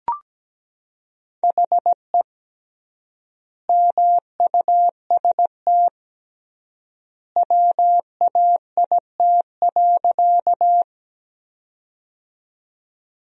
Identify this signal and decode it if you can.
Morse code, heard as a keyed tone, about 17 wpm, 700 Hz. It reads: HE MUST WAIT.
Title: 3x Word-Spacing - Sentences from Top 500 Words - 17wpm